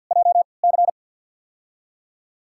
Morse code, heard as a keyed tone, about 49 wpm, spaced as for 28 wpm, 700 Hz. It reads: JC
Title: USA Call Signs - 5 Characters - 3x Character Spacing - 50wpm